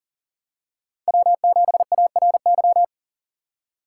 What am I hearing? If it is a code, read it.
Morse code, heard as a keyed tone, about 40 wpm, 700 Hz. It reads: W7ARY
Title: USA Call Signs - 5 Characters - 3x Character Spacing - 40wpm